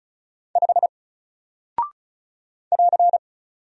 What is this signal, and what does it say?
Morse code, heard as a keyed tone, about 36 wpm, 700 Hz. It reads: 5 <AR>
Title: Single Character (Letter:Number:Punctuation:Prosign) Alt 35wpm